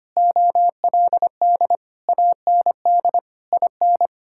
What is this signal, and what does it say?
Morse code, heard as a keyed tone, about 25 wpm, 700 Hz. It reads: OLD AND IN
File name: Sets of 9 Words - 25wpm